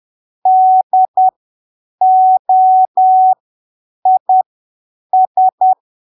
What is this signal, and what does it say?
Morse code, heard as a keyed tone, about 10 wpm, spaced as for 7 wpm, 750 Hz. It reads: DOIS
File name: arquivos / texto / texto 04 10ppm